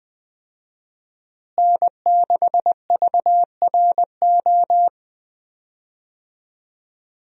Morse code, heard as a keyed tone, about 21 wpm, 700 Hz. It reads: N6VRO
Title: USA Call Signs - 5 Characters - Repeated 3x - 20wpm